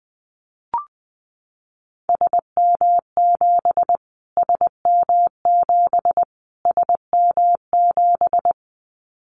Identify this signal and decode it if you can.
Morse code, heard as a keyed tone, about 20 wpm, 700 Hz. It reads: SM7 SM7 SM7
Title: Callsign Prefixes for Top 25 DXCC - Repeated 3x - 20wpm